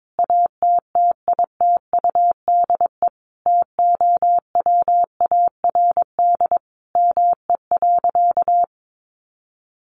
Morse code, heard as a keyed tone, about 22 wpm, 700 Hz. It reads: ATTITUDE TOWARD ME.
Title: Sentences from Top 1300 Words 22wpm